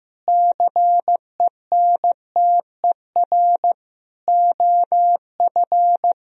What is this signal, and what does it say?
Morse code, heard as a keyed tone, about 15 wpm, 700 Hz. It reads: CENTER OF
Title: Sentences from Top 1000 Words - Review Part 2 - 15wpm